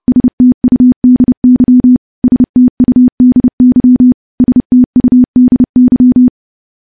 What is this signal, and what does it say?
Morse code, heard as a keyed tone, about 30 wpm, 250 Hz. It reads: STUDY STUDY STUDY
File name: morse / morse-training-30WPM